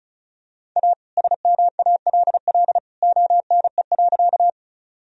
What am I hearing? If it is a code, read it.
Morse code, heard as a keyed tone, about 36 wpm, 700 Hz. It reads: A SMALL ONE.
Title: Sentences from Top 1000 Words - Review Part 1 - 35wpm